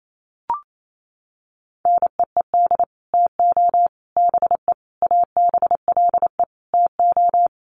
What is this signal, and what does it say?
Morse code, heard as a keyed tone, about 28 wpm, 700 Hz. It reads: NEED TO BE ABLE TO
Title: Sets of 5 Words Encore 28wpm